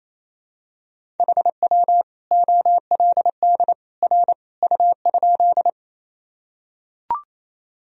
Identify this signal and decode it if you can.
Morse code, heard as a keyed tone, about 28 wpm, 700 Hz. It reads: HW OLD R U?
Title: Sentences from Top 200 Words and Abbreviations 28wpm